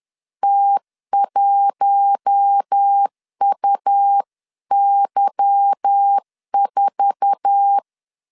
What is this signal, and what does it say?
Morse code, heard as a keyed tone, about 10 wpm, 800 Hz. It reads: T1UY4
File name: LEZIONI / 10 SIGNS / 10A SLASH 10wpm 2808